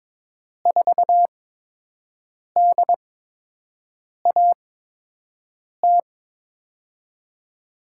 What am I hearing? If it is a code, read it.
Morse code, heard as a keyed tone, about 22 wpm, spaced as for 6 wpm, 700 Hz. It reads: 4DAT
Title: USA Call Signs - 8x Character Spacing - 22wpm